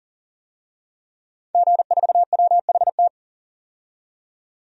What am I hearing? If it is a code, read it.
Morse code, heard as a keyed tone, about 40 wpm, 700 Hz. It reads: G4WHT
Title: DXCC-100 Call Signs - 5 Characters - 40wpm